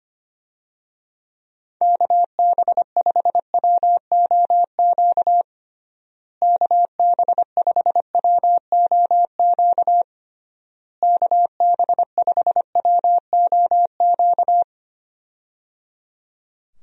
Morse code, heard as a keyed tone, about 25 wpm, 700 Hz. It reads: KB5WOQ KB5WOQ KB5WOQ
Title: USA Call Signs - 6 Characters - Repeated 3x - 25wpm